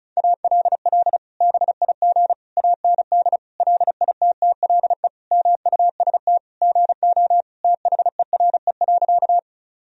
Morse code, heard as a keyed tone, about 36 wpm, 700 Hz. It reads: ALL BIG AND LITTLE MUST GO THERE.
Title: Sentences from Top 1000 Words - Review Part 1 - 35wpm